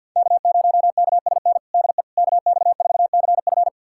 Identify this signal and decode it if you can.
Morse code, heard as a keyed tone, about 47 wpm, 690 Hz. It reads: K0KSN DE KX4CF
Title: USA Callsign DE Callsign 50wpm